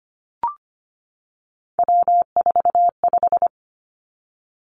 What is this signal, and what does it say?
Morse code, heard as a keyed tone, about 25 wpm, 700 Hz. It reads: W45
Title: Three Challenging Letter-Number Combinations 25wpm